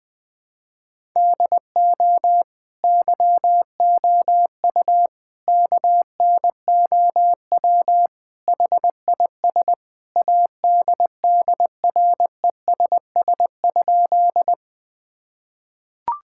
Morse code, heard as a keyed tone, about 20 wpm, 700 Hz. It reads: DO YOU KNOW HIS ADDRESS?
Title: Sentences from Top 1000 Words - Review Part 6 - 20wpm